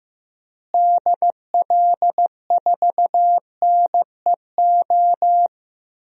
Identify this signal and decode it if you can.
Morse code, heard as a keyed tone, about 15 wpm, 700 Hz. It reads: DL4NEO
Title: DXCC-100 Call Signs - 6 Characters - Repeated 2x - 15wpm